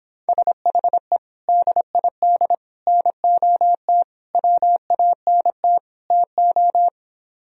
Morse code, heard as a keyed tone, about 26 wpm, 700 Hz. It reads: SHE DID NOT WANT TO